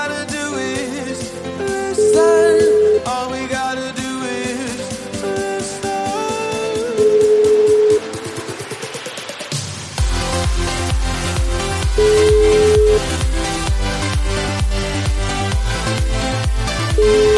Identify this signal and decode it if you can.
Ringback tone (Europe, 425 Hz), with noise.